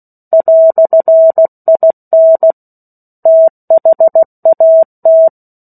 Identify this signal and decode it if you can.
Morse code, heard as a keyed tone, about 16 wpm, 650 Hz. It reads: "IN THAT
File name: 03-Chapter-03 16 WPM